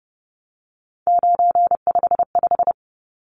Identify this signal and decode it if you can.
Morse code, heard as a keyed tone, about 30 wpm, 700 Hz. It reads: 955
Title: Numbers - 3-Digit 30wpm